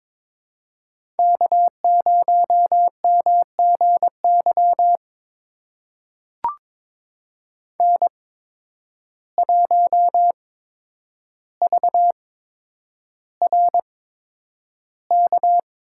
Morse code, heard as a keyed tone, about 22 wpm, spaced as for 9 wpm, 700 Hz. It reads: K0MGY N1VRK
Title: USA Call Signs - 5 Characters - 8x Character Spacing - 22wpm